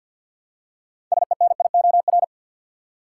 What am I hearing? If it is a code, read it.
Morse code, heard as a keyed tone, about 53 wpm, 700 Hz. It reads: SENIOR
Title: Six Letter Words - 50wpm